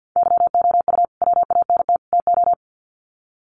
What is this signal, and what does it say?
Morse code, heard as a keyed tone, about 50 wpm, 700 Hz. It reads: YOU WANT TO
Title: Sets of 3 Words 50wpm